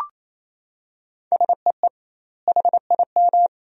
Morse code, heard as a keyed tone, about 28 wpm, 700 Hz. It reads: SEE HIM
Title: 2x Word-Spacing - Sentences from Top 300 Words - 28wpm